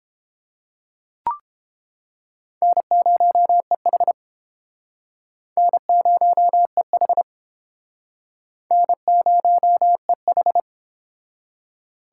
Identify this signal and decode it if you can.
Morse code, heard as a keyed tone, about 31 wpm, 700 Hz. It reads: N0EH N0EH N0EH